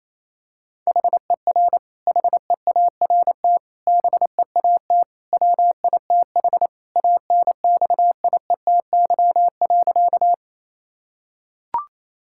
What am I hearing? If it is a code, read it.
Morse code, heard as a keyed tone, about 28 wpm, 700 Hz. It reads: HER HEART BEAT WITH ANXIETY.
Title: Sentences from Top 1600 Words 28wpm